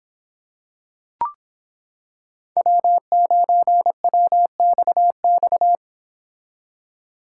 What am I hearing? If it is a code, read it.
Morse code, heard as a keyed tone, about 26 wpm, 700 Hz. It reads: W9WXX